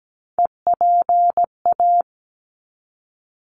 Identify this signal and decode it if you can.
Morse code, heard as a keyed tone, about 17 wpm, 700 Hz. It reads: EPA
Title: ARRL Sections 17wpm